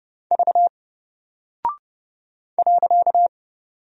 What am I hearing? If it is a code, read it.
Morse code, heard as a keyed tone, about 30 wpm, 700 Hz. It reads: V .
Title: Single Character (Letter:Number:Punctuation:Prosign) 30wpm